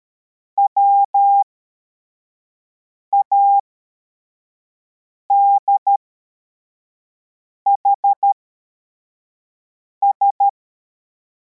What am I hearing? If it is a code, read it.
Morse code, heard as a keyed tone, about 13 wpm, spaced as for 4 wpm, 800 Hz. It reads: WADHS